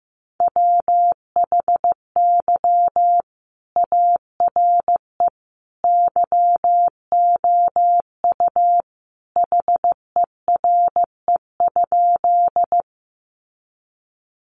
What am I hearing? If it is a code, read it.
Morse code, heard as a keyed tone, about 15 wpm, 700 Hz. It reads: WHY ARE YOU HERE?
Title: Sentences from Top 200 Words 15wpm